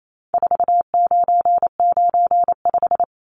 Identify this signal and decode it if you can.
Morse code, heard as a keyed tone, about 28 wpm, 700 Hz. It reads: 4995
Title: Numbers - 4-Digit 28wpm